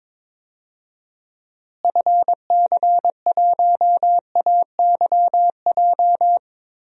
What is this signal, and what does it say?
Morse code, heard as a keyed tone, about 22 wpm, 700 Hz. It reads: FC1AYJ
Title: DXCC-100 Call Signs - 6 Characters - Repeated 3x - 22wpm